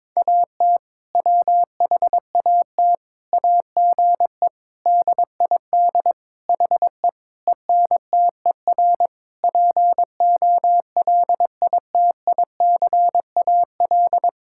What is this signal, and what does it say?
Morse code, heard as a keyed tone, about 22 wpm, 700 Hz. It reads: AT WHAT AGE DID HE ENTER POLITICAL
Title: Sentences from Top 1000 Words - Review Part 5 - 22wpm